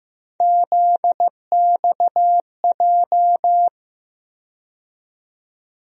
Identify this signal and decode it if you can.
Morse code, heard as a keyed tone, about 15 wpm, 700 Hz. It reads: ZXJ
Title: USA Call Sign Postfixes - 3 Letters 15wpm